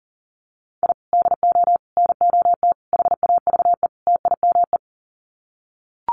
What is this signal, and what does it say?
Morse code, heard as a keyed tone, about 40 wpm, 700 Hz. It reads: I DO NOT HAVE TIME
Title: Sets of 5 Words Encore 40wpm